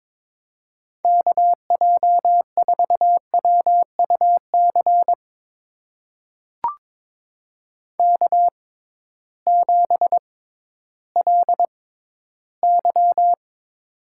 Morse code, heard as a keyed tone, about 22 wpm, spaced as for 13 wpm, 700 Hz. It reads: KJ4WUC K7LY
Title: USA Call Signs - 6x Character Spacing - 22wpm